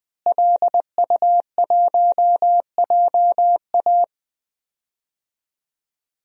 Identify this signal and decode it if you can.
Morse code, heard as a keyed tone, about 20 wpm, 700 Hz. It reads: LU1JA
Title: DXCC-100 Call Signs - 5 Characters - Repeated 3x - 20wpm